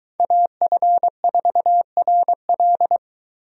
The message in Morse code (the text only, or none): AF4RL